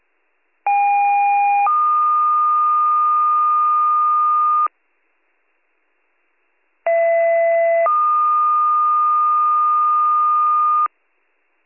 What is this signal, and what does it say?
Two-tone page, heard as a one-second tone, then a three-second tone, about 810 then 1220 Hz, about 670 then 1220 Hz. It is heard from 0.7 to 10.9 s.